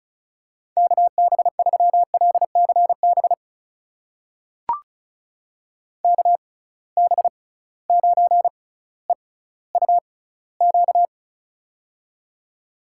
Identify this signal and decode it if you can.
Morse code, heard as a keyed tone, about 34 wpm, spaced as for 18 wpm, 700 Hz. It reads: KB3LCB KB9EUQ
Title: USA Call Signs - 6 Characters - 6x Character Spacing - 35wpm